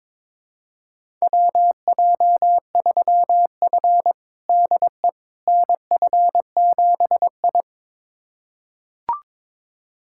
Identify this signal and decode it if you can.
Morse code, heard as a keyed tone, about 22 wpm, 700 Hz. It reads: WJ3F DE NF7I
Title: USA Callsign DE Callsign - 4 Characters - 22wpm